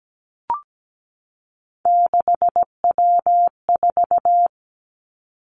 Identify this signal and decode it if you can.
Morse code, heard as a keyed tone, about 17 wpm, 700 Hz. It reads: 6W4
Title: Three Challenging Letter-Number Combinations 17wpm